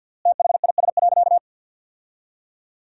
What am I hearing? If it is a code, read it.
Morse code, heard as a keyed tone, about 50 wpm, 700 Hz. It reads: THIS.
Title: 3x Word-Spacing - Sentences from Top 100 Words - 50wpm